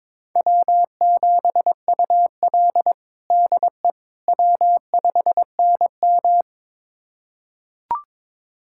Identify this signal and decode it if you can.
Morse code, heard as a keyed tone, about 22 wpm, 700 Hz. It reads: W7UL DE W5NM
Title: USA Callsign DE Callsign - 4 Characters - 22wpm